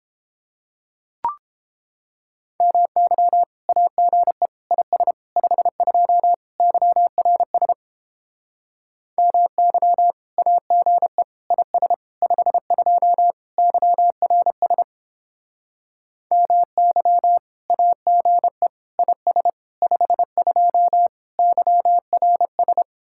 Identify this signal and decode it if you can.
Morse code, heard as a keyed tone, about 29 wpm, 700 Hz. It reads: MY AGE IS 52 YRS MY AGE IS 52 YRS MY AGE IS 52 YRS